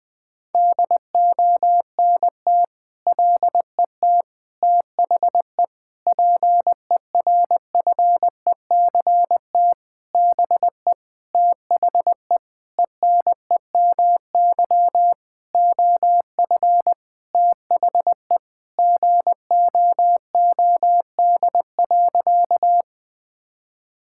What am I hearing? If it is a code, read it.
Morse code, heard as a keyed tone, about 20 wpm, 700 Hz. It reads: DONT LET THE PERFECT BE THE ENEMY OF THE GOOD.
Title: Sentences with 9-Word Phrases - 20wpm